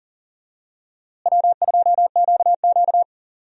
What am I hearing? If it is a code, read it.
Morse code, heard as a keyed tone, about 40 wpm, 700 Hz. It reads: W2QQ
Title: USA Call Signs - 4 Characters - Repeated 3x - 40wpm